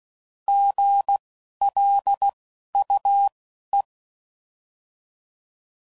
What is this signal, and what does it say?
Morse code, heard as a keyed tone, about 16 wpm, spaced as for 11 wpm, 800 Hz. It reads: GLUE
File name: Prefixes / By First Letter